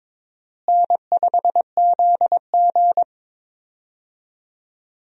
Morse code, heard as a keyed tone, about 22 wpm, 700 Hz. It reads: N5ZG
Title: USA Call Signs - 4 Characters - Repeated 2x - 22wpm